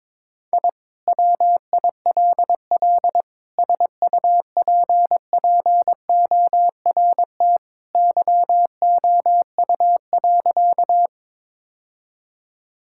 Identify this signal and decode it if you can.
Morse code, heard as a keyed tone, about 22 wpm, 700 Hz. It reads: I WILL SUPPORT YOU.
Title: Sentences from Top 1000 Words - Review Part 5 - 22wpm